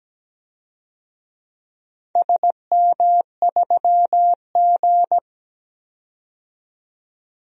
Morse code, heard as a keyed tone, about 17 wpm, 700 Hz. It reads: SM3G
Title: DXCC-100 Call Signs - 4 Characters - 17wpm